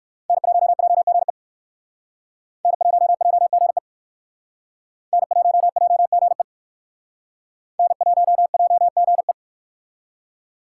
Morse code, heard as a keyed tone, about 56 wpm, 700 Hz. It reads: N1JGE N1JGE N1JGE N1JGE